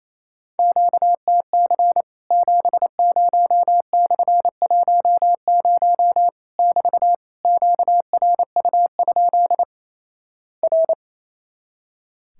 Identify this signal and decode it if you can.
Morse code, heard as a keyed tone, about 28 wpm, 700 Hz. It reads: QTC 70/10 <BT> QRU? R
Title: Worked All Europe DX Contest - QTC Exchange - 28wpm